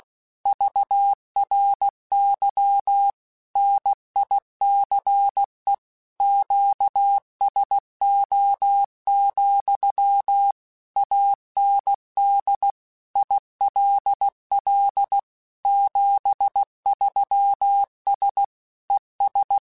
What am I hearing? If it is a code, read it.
Morse code, heard as a keyed tone, about 16 wpm, 800 Hz. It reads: VRY NICE QSO, AND ILL 73S ES
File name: QSOs / QSOs-mp3s / kc7sac@15